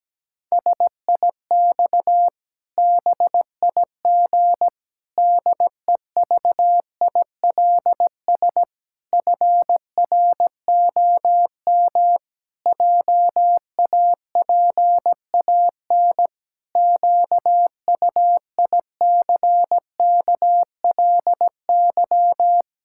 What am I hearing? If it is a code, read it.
Morse code, heard as a keyed tone, about 17 wpm, 700 Hz. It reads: SIX BIG DEVILS FROM JAPAN QUICKLY
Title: Pangram Challenge v2 - 17wpm